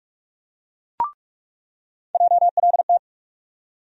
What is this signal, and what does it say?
Morse code, heard as a keyed tone, about 42 wpm, 700 Hz. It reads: JLT
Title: USA Call Sign Postfixes - 3 Letters 45wpm